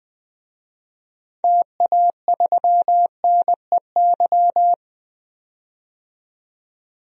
Morse code, heard as a keyed tone, about 20 wpm, 700 Hz. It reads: TA3NEY
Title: DXCC-100 Call Signs - 6 Characters - 20wpm